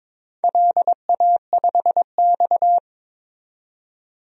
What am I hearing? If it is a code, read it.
Morse code, heard as a keyed tone, about 21 wpm, 700 Hz. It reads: LA5X
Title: DXCC-100 Call Signs - 4 Characters - 22wpm